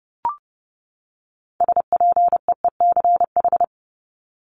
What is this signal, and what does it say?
Morse code, heard as a keyed tone, about 30 wpm, 700 Hz. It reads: SPEECH
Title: Top 1751-2000 Words 30wpm